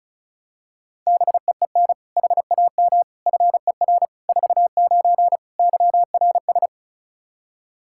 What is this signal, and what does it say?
Morse code, heard as a keyed tone, about 36 wpm, 700 Hz. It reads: BEEN HAM FER 49 YRS
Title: Practice QSO - Exchange 3 - More Info - 35wpm